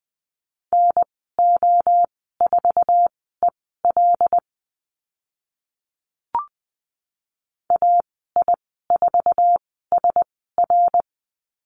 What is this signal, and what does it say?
Morse code, heard as a keyed tone, about 20 wpm, spaced as for 14 wpm, 700 Hz. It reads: NO4EL AI4SR
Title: 082 Lesson 14g - Number 2 - Call Signs 20wpm